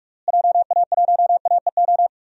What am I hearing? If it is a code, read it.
Morse code, heard as a keyed tone, about 46 wpm, 690 Hz. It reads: JA1AEO
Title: DXCC-100 Call Signs - 6 Characters - Repeated 2x - 45wpm